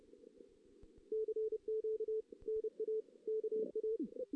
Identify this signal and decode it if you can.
Morse code, heard as a keyed tone, about 30 wpm, 440 Hz. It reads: CQ NA KA
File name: 2016 JAN CW NAQP / 2016 Jan naqp k5ot - 0-2